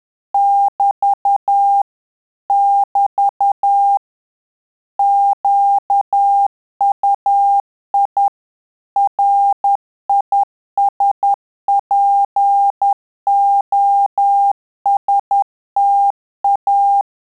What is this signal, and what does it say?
Morse code, heard as a keyed tone, about 10 wpm, 800 Hz. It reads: <BT> <BT> QUI RISPOSTA
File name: Telegraphy-group / QSO CW 10 PPM